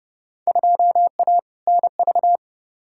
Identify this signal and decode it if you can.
Morse code, heard as a keyed tone, about 30 wpm, 700 Hz. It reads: 2A NV
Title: Field Day Exchanges - 30wpm